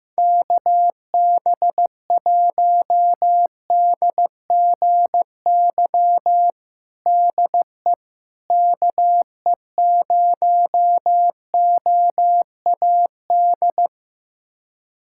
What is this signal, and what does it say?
Morse code, heard as a keyed tone, about 15 wpm, 700 Hz. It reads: KB1DGY DE KE0OAD
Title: USA Callsign DE Callsign - 6 Characters - 15wpm